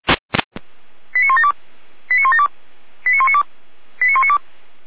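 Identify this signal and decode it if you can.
Five-tone selective call (ZVEI), heard as quick runs of five tones, tones of about 70 ms. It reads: 89172 89172 89192 89192
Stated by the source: Five-tone ZVEI selective call 89391, received on an SDR.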